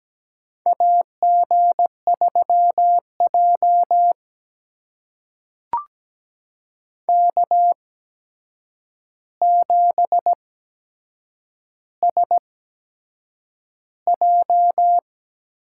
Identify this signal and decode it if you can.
Morse code, heard as a keyed tone, about 17 wpm, spaced as for 7 wpm, 700 Hz. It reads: AG3J K7SJ
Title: USA Call Signs - 4 Characters - 8x Character Spacing - 17wpm